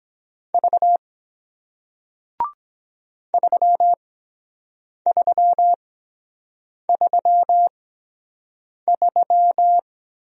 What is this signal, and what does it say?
Morse code, heard as a keyed tone, about 23 wpm, spaced as for 7 wpm, 700 Hz. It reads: V 3333